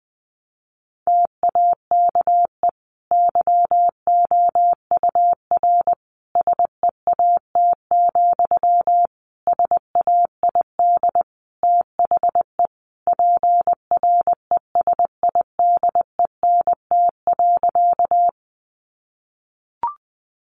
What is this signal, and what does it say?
Morse code, heard as a keyed tone, about 20 wpm, 700 Hz. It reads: TAKE YOUR SEAT, SAID THE PRESIDENT.
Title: Sentences from Top 1500 Words 20wpm